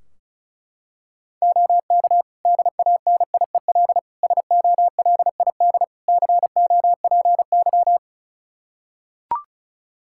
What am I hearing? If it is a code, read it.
Morse code, heard as a keyed tone, about 36 wpm, 700 Hz. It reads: OK DANIEL SOLID COPY
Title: K1USN SST QSO Practice 35wpm